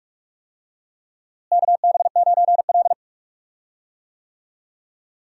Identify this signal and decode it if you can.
Morse code, heard as a keyed tone, about 43 wpm, 700 Hz. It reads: KB9L
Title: USA Call Signs - 4 Characters - Repeated 2x - 45wpm